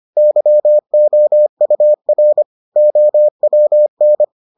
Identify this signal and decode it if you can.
Morse code, heard as a keyed tone, about 25 wpm, 600 Hz. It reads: YOUR OWN